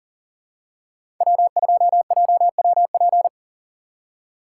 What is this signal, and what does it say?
Morse code, heard as a keyed tone, about 40 wpm, 700 Hz. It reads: W2JWP
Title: USA Call Signs - 5 Characters - 3x Character Spacing - 40wpm